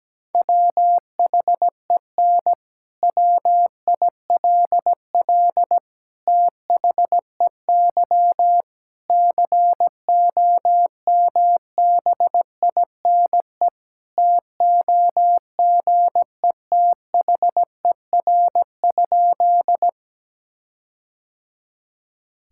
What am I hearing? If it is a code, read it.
Morse code, heard as a keyed tone, about 17 wpm, 700 Hz. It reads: WHEN WILL THEY COMBINE TOGETHER?
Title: Sentences from Top 1600 Words 17wpm